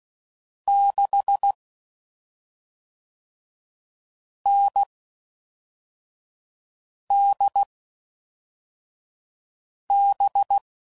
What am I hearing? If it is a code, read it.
Morse code, heard as a keyed tone, about 16 wpm, 800 Hz. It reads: 6 N D B
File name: Combinations / Difficult Combos / NDB NDB6